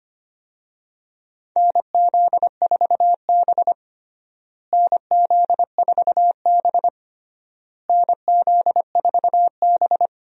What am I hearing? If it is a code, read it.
Morse code, heard as a keyed tone, about 25 wpm, 700 Hz. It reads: NZ4B NZ4B NZ4B
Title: USA Call Signs - 4 Characters - Repeated 3x - 25wpm